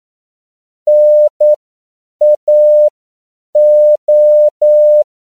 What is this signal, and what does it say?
Morse code, heard as a keyed tone, about 9 wpm, spaced as for 7 wpm, 600 Hz. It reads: NAO
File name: CW / 10PPM